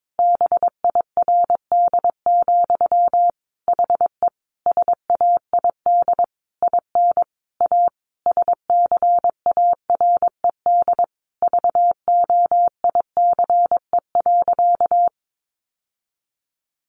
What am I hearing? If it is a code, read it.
Morse code, heard as a keyed tone, about 22 wpm, 700 Hz. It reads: BIRD, HE SAID IN A SCARED VOICE.
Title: Sentences from Top 700 Words 22wpm